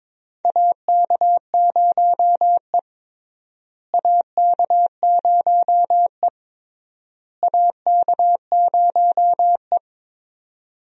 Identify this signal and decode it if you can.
Morse code, heard as a keyed tone, about 22 wpm, 700 Hz. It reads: AK0E AK0E AK0E
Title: USA Call Signs - 4 Characters - Repeated 3x - 22wpm